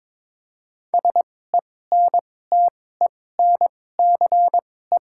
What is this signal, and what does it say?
Morse code, heard as a keyed tone, about 21 wpm, spaced as for 16 wpm, 700 Hz. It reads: SENTENCE
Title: 2x Character Spacing - Top 300 Words - 22wpm